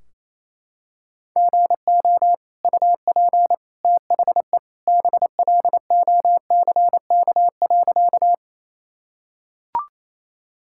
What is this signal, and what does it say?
Morse code, heard as a keyed tone, about 28 wpm, 700 Hz. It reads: GO UP THE BLOCK.
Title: Sentences from Top 1600 Words 28wpm